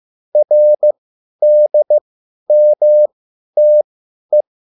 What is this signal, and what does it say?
Morse code, heard as a keyed tone, about 15 wpm, spaced as for 10 wpm, 600 Hz. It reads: RDMTE